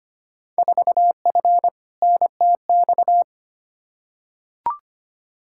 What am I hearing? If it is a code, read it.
Morse code, heard as a keyed tone, about 25 wpm, 700 Hz. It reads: 4F NTX
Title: Field Day Exchanges Encore - 25wpm